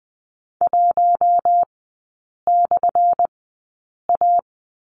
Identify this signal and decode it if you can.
Morse code, heard as a keyed tone, about 20 wpm, spaced as for 8 wpm, 700 Hz. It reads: 1/A
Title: 218 Lesson 35e - Number 0 - Three Characters 20wpm